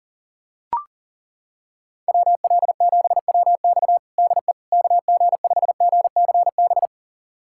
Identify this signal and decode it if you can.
Morse code, heard as a keyed tone, about 40 wpm, 700 Hz. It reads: WL7WX DE KG5GCB
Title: USA Callsign DE Callsign 40wpm